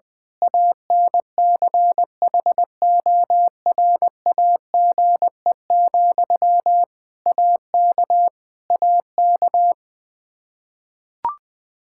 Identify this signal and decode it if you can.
Morse code, heard as a keyed tone, about 20 wpm, 700 Hz. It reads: ANCHORAGE, AK AK
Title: Practice QSO - Exchange 1 - Beginning - 20wpm